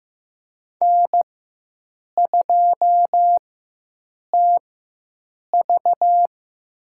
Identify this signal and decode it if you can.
Morse code, heard as a keyed tone, about 15 wpm, spaced as for 7 wpm, 700 Hz. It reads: N2TV
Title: USA Call Signs - 4 Characters - 4x Character Spacing - 15wpm